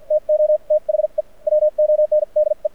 Morse code, heard as a keyed tone, about 56 wpm, spaced as for 44 wpm, 600 Hz. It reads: TO THE WONDE
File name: archived / CRSnet